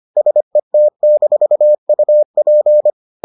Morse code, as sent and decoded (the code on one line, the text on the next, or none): ... - -....- ..- .--.
SET-UP